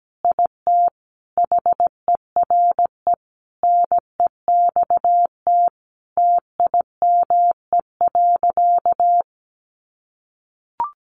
Morse code, heard as a keyed tone, about 17 wpm, 700 Hz. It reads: IT HERE NEXT TIME.
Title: Sentences from Top 1500 Words 17wpm